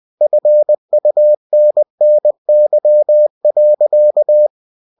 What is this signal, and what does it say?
Morse code, heard as a keyed tone, about 20 wpm, 600 Hz. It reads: FUNNY.
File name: cw-books / winnie-the-pooh / winnie-20wpm-page-0039